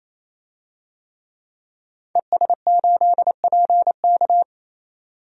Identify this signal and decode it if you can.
Morse code, heard as a keyed tone, about 29 wpm, 690 Hz. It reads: ES8PK
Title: DXCC-100 Call Signs - 5 Characters - 28wpm